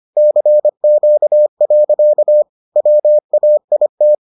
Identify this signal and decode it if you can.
Morse code, heard as a keyed tone, about 25 wpm, 600 Hz. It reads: CQ. WAIT